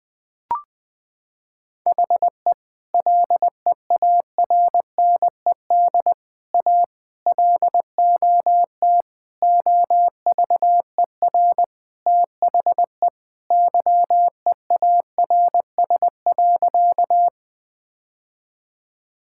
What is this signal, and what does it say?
Morse code, heard as a keyed tone, about 20 wpm, 700 Hz. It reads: HE LEARNED A LOT OVER THE YEARS.
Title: Sentences with 3-Word Phrases - 20wpm